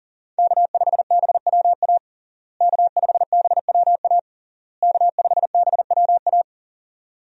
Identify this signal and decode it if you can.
Morse code, heard as a keyed tone, about 40 wpm, 700 Hz. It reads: K5BWA K5BWA K5BWA
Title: USA Call Signs - 5 Characters - Repeated 3x - 40wpm